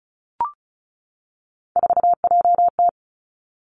Morse code, heard as a keyed tone, about 36 wpm, 700 Hz. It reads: 4JT
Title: Three Challenging Letter-Number Combinations 35wpm